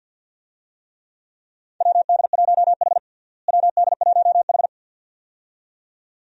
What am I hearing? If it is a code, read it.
Morse code, heard as a keyed tone, about 49 wpm, 700 Hz. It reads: WD1H WD1H
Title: USA Call Signs - 4 Characters - Repeated 2x - 50wpm